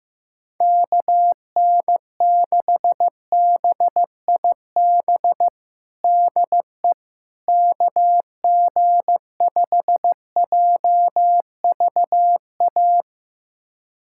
Morse code, heard as a keyed tone, about 15 wpm, 700 Hz. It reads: KN6BIB DE KG5JVA
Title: USA Callsign DE Callsign - 6 Characters - 15wpm